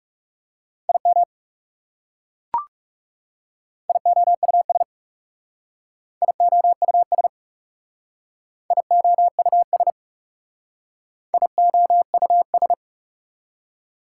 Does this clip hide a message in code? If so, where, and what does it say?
Yes: Morse code from 0.9 to 12.7 s, heard as a keyed tone, about 39 wpm, 700 Hz. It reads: IM IOUS IOUS IOUS IOUS